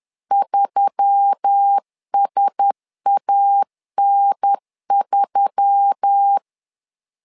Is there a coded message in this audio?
Yes: Morse code from 0.3 to 6.4 s, heard as a keyed tone, about 10 wpm, 800 Hz. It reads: 3SAN3